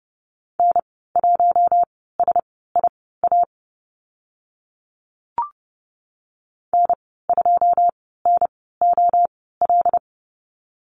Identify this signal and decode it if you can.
Morse code, heard as a keyed tone, about 30 wpm, spaced as for 17 wpm, 700 Hz. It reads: N1SIA N2NOL
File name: Lesson 14g - Number 2 - Call Signs 30wpm